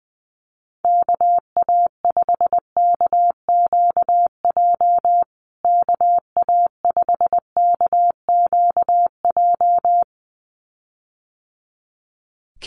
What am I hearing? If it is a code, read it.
Morse code, heard as a keyed tone, about 20 wpm, 700 Hz. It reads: KA5KQJ KA5KQJ
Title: USA Call Signs Repeated 2x 20wpm